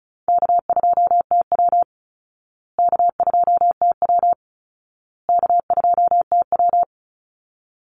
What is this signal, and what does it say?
Morse code, heard as a keyed tone, about 34 wpm, 700 Hz. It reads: K2TW K2TW K2TW
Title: Easy Call Signs Repeated 3x - 35wpm